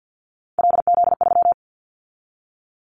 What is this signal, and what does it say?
Morse code, heard as a keyed tone, about 50 wpm, 690 Hz. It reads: LZ3
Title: Callsign Prefixes for Top 100 DXCC - 50wpm